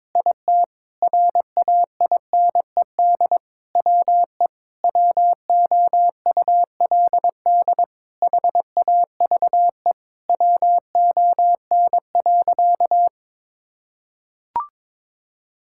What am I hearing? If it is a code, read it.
Morse code, heard as a keyed tone, about 22 wpm, 700 Hz. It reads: IT RAINED WE WOULD HAVE WON.
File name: Sentences with 7-Word Phrases - 22wpm